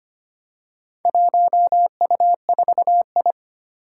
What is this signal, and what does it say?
Morse code, heard as a keyed tone, about 25 wpm, 700 Hz. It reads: 1U4I
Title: Four Challenging Letter-Number Combinations 25wpm